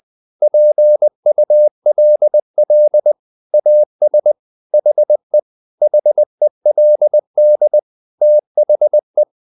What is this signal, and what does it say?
Morse code, heard as a keyed tone, about 20 wpm, 600 Hz. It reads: PULL AS HE HELD THE
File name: the old man and the sea / the old man and the sea part 1 20wpm